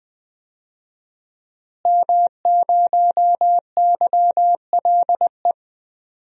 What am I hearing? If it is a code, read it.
Morse code, heard as a keyed tone, about 20 wpm, 700 Hz. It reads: M0YLE
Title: DXCC-100 Call Signs - 5 Characters - Repeated 2x - 20wpm